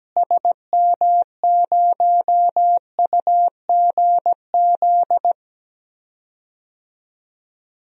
Morse code, heard as a keyed tone, about 17 wpm, 700 Hz. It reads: SM0UGZ
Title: DXCC-100 Call Signs - 6 Characters - Repeated 3x - 17wpm